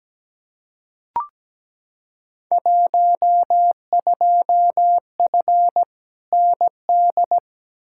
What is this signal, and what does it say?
Morse code, heard as a keyed tone, about 17 wpm, 700 Hz. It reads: 12F ND
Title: Field Day Exchanges Encore - 17wpm